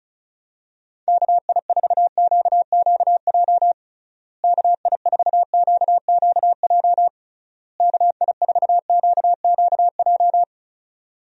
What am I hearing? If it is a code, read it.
Morse code, heard as a keyed tone, about 34 wpm, 700 Hz. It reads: KI4QQJ KI4QQJ KI4QQJ
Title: USA Call Signs - 6 Characters - Repeated 3x - 35wpm